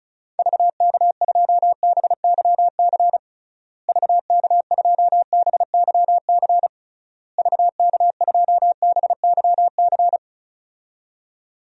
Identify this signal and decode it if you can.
Morse code, heard as a keyed tone, about 36 wpm, 700 Hz. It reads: VK2BYC VK2BYC VK2BYC
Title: DXCC-100 Call Signs - 6 Characters - Repeated 3x - 35wpm